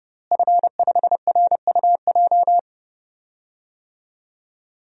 Morse code, heard as a keyed tone, about 30 wpm, 700 Hz. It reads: F5RUJ
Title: DXCC-100 Call Signs - 5 Characters - Repeated 3x - 30wpm